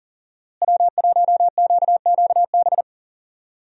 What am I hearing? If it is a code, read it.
Morse code, heard as a keyed tone, about 40 wpm, 700 Hz. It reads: W1QQB
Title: USA Call Signs - 5 Characters - Repeated 2x - 40wpm